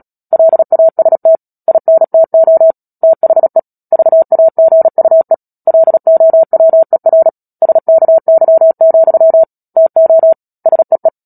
Morse code, heard as a keyed tone, about 37 wpm, 650 Hz. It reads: LAST INTO THE VAGUE LOWER SKY, TO SEE
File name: Book Two / 10-Chapter 10 - Epilogue - 36 WPM